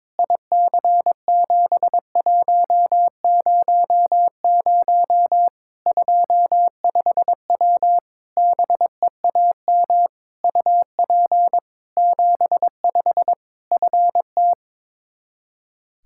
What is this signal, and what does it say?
Morse code, heard as a keyed tone, about 22 wpm, 700 Hz. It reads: IC7100 25W BEAM UP 75 FT
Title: Practice QSO - Exchange 2 - Info - 22wpm